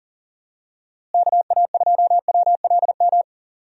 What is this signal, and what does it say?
Morse code, heard as a keyed tone, about 40 wpm, 700 Hz. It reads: KA2WLM
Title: USA Call Signs - 6 Characters - Repeated 3x - 40wpm